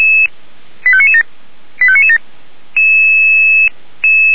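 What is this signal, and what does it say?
Five-tone selective call (ZVEI), heard as quick runs of five tones, tones of about 70 ms. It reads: 85997 85997
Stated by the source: ZVEI five-tone selective call received on an SDR.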